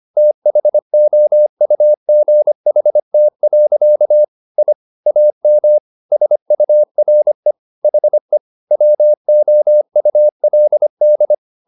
Morse code, heard as a keyed tone, about 25 wpm, 600 Hz. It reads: THOUGHT. I AM SURE HE WOULD
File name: the old man and the sea part 1 25wpm